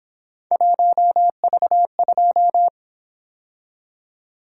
Morse code, heard as a keyed tone, about 26 wpm, 700 Hz. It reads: 1V2
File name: Speed-Racing - Three Letter-Number Combinations - 22wpm